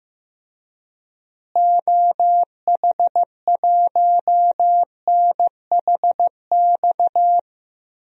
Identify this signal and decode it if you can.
Morse code, heard as a keyed tone, about 15 wpm, 700 Hz. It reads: OH1NHX
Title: DXCC-100 Call Signs - 6 Characters - Repeated 3x - 15wpm